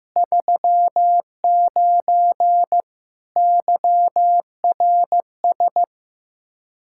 Morse code, heard as a keyed tone, about 15 wpm, 700 Hz. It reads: 39 YRS
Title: Practice QSO - Exchange 3 - More Info - 15wpm